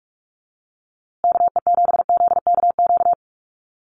Morse code, heard as a keyed tone, about 44 wpm, 700 Hz. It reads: KE7ZKQ
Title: USA Call Signs Repeated 2x 45wpm